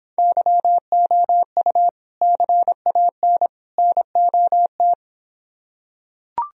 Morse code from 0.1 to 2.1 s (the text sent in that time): YOU